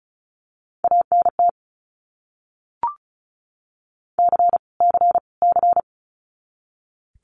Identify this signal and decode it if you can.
Morse code, heard as a keyed tone, about 34 wpm, 700 Hz. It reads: ANT C C C
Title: Abbreviations Repeated 3x - 35wpm